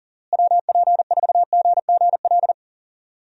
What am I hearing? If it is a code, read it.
Morse code, heard as a keyed tone, about 40 wpm, 700 Hz. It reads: WP4GGL
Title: USA Call Signs - 6 Characters - Repeated 3x - 40wpm